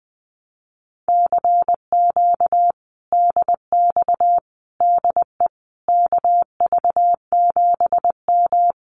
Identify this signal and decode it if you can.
Morse code, heard as a keyed tone, about 20 wpm, 700 Hz. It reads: CQ DX DE KV7M
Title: Common QSO Phrases Encore 20wpm